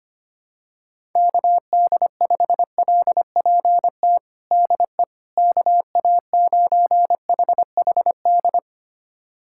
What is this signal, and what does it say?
Morse code, heard as a keyed tone, about 25 wpm, 700 Hz. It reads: KD5LPT DE KA9HHD
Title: USA Callsign DE Callsign 25wpm